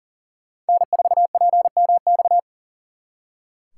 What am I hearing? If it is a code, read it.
Morse code, heard as a keyed tone, about 52 wpm, 700 Hz. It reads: N4PMX